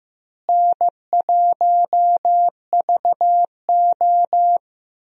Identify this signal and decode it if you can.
Morse code, heard as a keyed tone, about 15 wpm, 700 Hz. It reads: N1VO
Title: USA Call Signs - 4 Characters - Repeated 2x - 15wpm